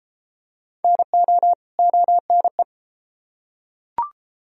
Morse code, heard as a keyed tone, about 32 wpm, 700 Hz. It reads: NO ONE